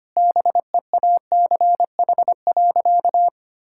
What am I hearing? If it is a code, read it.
Morse code, heard as a keyed tone, about 25 wpm, 700 Hz. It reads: BEACH.
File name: Easy Sentences with Hints - 25wpm